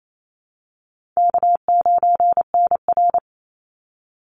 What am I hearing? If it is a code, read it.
Morse code, heard as a keyed tone, about 28 wpm, 700 Hz. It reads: K9NR
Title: Easy Call Signs - 28wpm